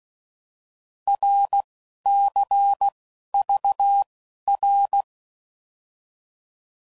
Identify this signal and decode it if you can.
Morse code, heard as a keyed tone, about 16 wpm, spaced as for 12 wpm, 800 Hz. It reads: RCVR
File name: Mixed / Abreviations / Abreviations